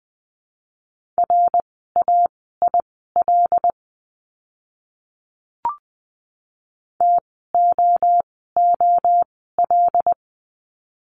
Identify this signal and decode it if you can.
Morse code, heard as a keyed tone, about 20 wpm, spaced as for 15 wpm, 700 Hz. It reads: RAIL TOOL
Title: 075 Lesson 13f - Letter L - Words 20wpm